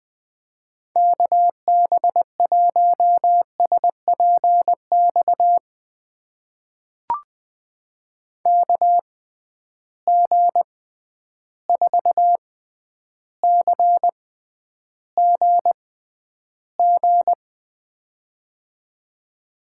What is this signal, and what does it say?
Morse code, heard as a keyed tone, about 20 wpm, spaced as for 10 wpm, 700 Hz. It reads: KB1SPX KG4CGG
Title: USA Call Signs - 6 Characters - 6x Character Spacing - 20wpm